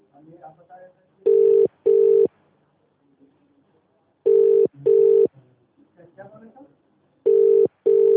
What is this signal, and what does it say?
Ringback tone (UK), 400 + 450 Hz, with speech.